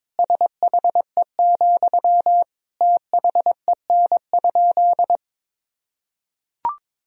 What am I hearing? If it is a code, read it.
Morse code, heard as a keyed tone, about 21 wpm, 700 Hz. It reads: SHE, THEN?
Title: Sentences from Top 200 Words and Abbreviations 22wpm